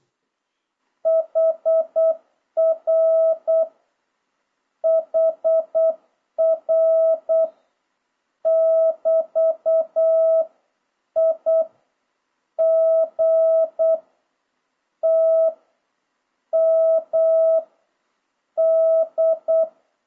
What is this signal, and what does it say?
Morse code, heard as a keyed tone, about 8 wpm, 650 Hz. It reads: HR HR <BT> I G T M D
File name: jh8cbh / 2026cw